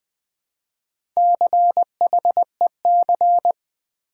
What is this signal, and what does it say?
Morse code, heard as a keyed tone, about 20 wpm, 700 Hz. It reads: CHEC
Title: Build A Word - 1 Letter At A Time - 3 to 5-Letter Words - 20wpm